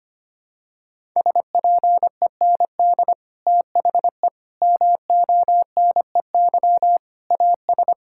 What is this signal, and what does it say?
Morse code, heard as a keyed tone, about 25 wpm, 700 Hz. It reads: SPEND THE MONEY AS
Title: Sentences from Top 1000 Words - Review Part 5 - 25wpm